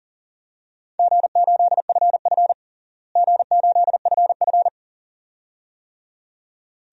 Morse code, heard as a keyed tone, about 40 wpm, 700 Hz. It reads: G8FF G8FF
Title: DXCC-100 Call Signs - 4 Characters - Repeated 2x - 40wpm